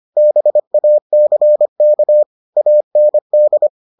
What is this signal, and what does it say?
Morse code, heard as a keyed tone, about 25 wpm, 600 Hz. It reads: BACK AND